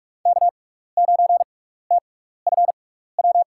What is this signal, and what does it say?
Morse code, heard as a keyed tone, about 42 wpm, spaced as for 15 wpm, 700 Hz. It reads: K9TFW
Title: USA Call Signs - 6x Character Spacing - 45wpm